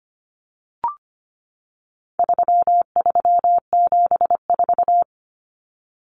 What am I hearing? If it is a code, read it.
Morse code, heard as a keyed tone, about 25 wpm, 700 Hz. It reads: 3374
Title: Numbers - 4-Digit 25wpm